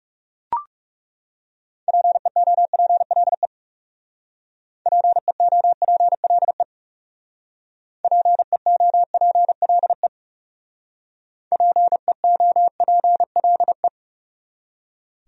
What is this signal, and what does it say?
Morse code, heard as a keyed tone, about 39 wpm, 700 Hz. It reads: PEOPLE PEOPLE PEOPLE PEOPLE